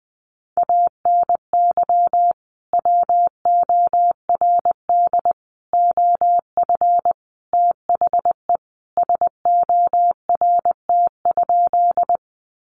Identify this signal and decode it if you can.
Morse code, heard as a keyed tone, about 20 wpm, 700 Hz. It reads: ANY WORD OF THE SORT?
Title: Sentences from Top 1300 Words 20wpm